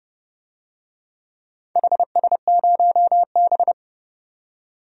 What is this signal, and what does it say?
Morse code, heard as a keyed tone, about 32 wpm, 700 Hz. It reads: HS0B
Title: DXCC-100 Call Signs - 4 Characters - 30wpm